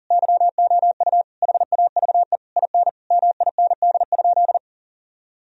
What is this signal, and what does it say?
Morse code, heard as a keyed tone, about 40 wpm, 700 Hz. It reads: YOU HAVE IN MIND?
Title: Sentences from Top 1000 Words - Review Part 4 - 40wpm